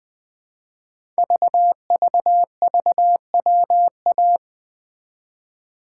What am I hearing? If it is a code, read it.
Morse code, heard as a keyed tone, about 20 wpm, 700 Hz. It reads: VVVWA
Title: Code Groups - 5 Letters - 20wpm